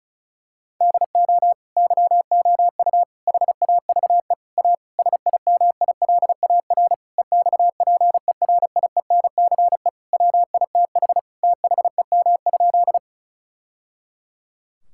Morse code, heard as a keyed tone, about 34 wpm, 700 Hz. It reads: DO YOU HAVE A SIMILAR EXPERIENCE WITH THEM?
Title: Sentences from Top 1000 Words - Review Part 5 - 35wpm